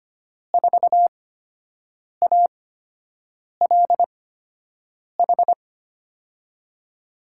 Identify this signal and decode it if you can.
Morse code, heard as a keyed tone, about 25 wpm, spaced as for 7 wpm, 700 Hz. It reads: 4ALH
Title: USA Call Signs - 6 Characters - 8x Character Spacing - 25wpm